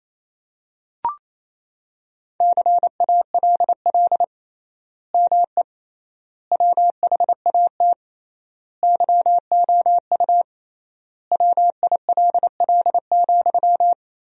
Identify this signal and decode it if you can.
Morse code, heard as a keyed tone, about 28 wpm, 700 Hz. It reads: CALL ME WHAT YOU WILL,
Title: 3x Word-Spacing - Sentences from Top 200 Words - 28wpm